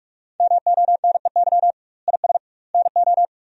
Morse code, heard as a keyed tone, about 41 wpm, 700 Hz. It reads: MONEY IS NO
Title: 2x Word-Spacing - Sentences from Top 500 Words - 45wpm